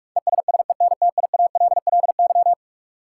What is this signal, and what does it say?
Morse code, heard as a keyed tone, about 46 wpm, 700 Hz. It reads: ESSENTIALLY
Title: Sentences from Top 1300 Words - New Word Drill Alt - 45wpm